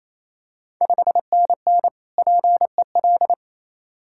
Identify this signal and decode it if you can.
Morse code, heard as a keyed tone, about 28 wpm, 690 Hz. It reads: 5NN PEL
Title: Ontario QSO Party Exchange 28wpm